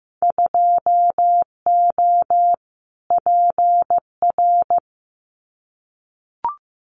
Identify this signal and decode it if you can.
Morse code, heard as a keyed tone, about 15 wpm, 700 Hz. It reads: 2O PR
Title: Winter Field Day Exchanges Encore 15wpm